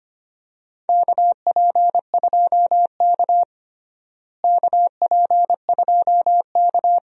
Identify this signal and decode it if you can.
Morse code, heard as a keyed tone, about 25 wpm, 700 Hz. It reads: KP2K KP2K
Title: DXCC-100 Call Signs - 4 Characters - Repeated 2x - 25wpm